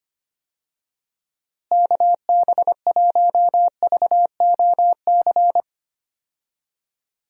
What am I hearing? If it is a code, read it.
Morse code, heard as a keyed tone, about 25 wpm, 700 Hz. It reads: KB1VOC
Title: USA Call Signs - 6 Characters - Repeated 2x - 25wpm